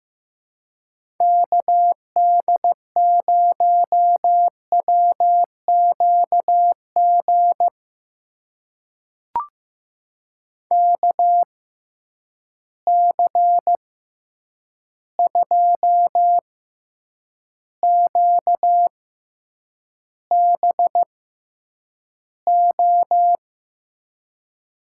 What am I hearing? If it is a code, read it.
Morse code, heard as a keyed tone, about 15 wpm, spaced as for 8 wpm, 700 Hz. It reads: KD0WQG KC2QBO
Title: USA Call Signs - 6 Characters - 6x Character Spacing - 15wpm